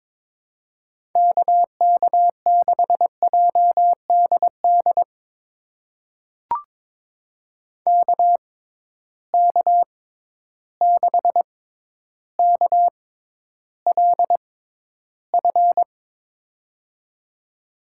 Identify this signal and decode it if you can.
Morse code, heard as a keyed tone, about 22 wpm, spaced as for 11 wpm, 700 Hz. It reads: KK6JDD KK6KLF
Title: USA Call Signs - 6 Characters - 6x Character Spacing - 22wpm